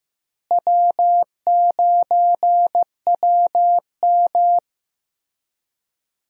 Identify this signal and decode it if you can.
Morse code, heard as a keyed tone, about 16 wpm, 700 Hz. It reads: W9WM
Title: USA Call Signs - 4 Characters - Repeated 3x - 15wpm